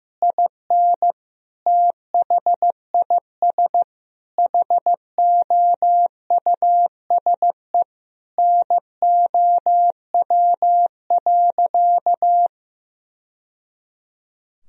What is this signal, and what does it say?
Morse code, heard as a keyed tone, about 15 wpm, 700 Hz. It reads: IN THIS HOUSE NOW.
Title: Sentences from Top 1000 Words - Review Part 2 - 15wpm